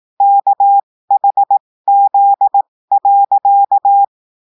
Morse code, heard as a keyed tone, about 18 wpm, spaced as for 15 wpm, 800 Hz. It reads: KHZ.
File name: cours-du-lundi files / cours-1580-texte-chiffre-15-mpm